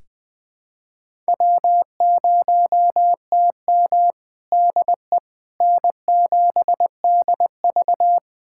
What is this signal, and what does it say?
Morse code, heard as a keyed tone, about 20 wpm, 700 Hz. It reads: W0TM DE N7DV
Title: USA Callsign DE Callsign - 4 Characters - 20wpm